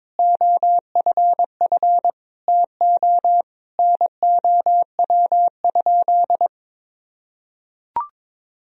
Morse code, heard as a keyed tone, about 22 wpm, 700 Hz. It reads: OFF TO NOW?
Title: Sentences from Top 200 Words and Abbreviations 22wpm